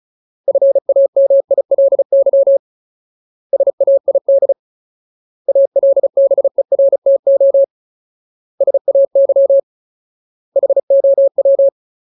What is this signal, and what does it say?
Morse code, heard as a keyed tone, about 34 wpm, 550 Hz. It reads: FAMILY SAID ALBERTO SAY HOW
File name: Ultimate Word List / 35WPM Ultimate